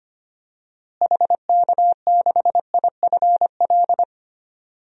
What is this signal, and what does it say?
Morse code, heard as a keyed tone, about 25 wpm, 700 Hz. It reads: HK6IFL
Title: DXCC-100 Call Signs - 6 Characters - Repeated 2x - 25wpm